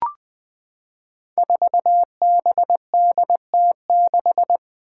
Morse code, heard as a keyed tone, about 20 wpm, 700 Hz. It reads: 4BDT6
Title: Five Challenging Letter-Number Combinations 20wpm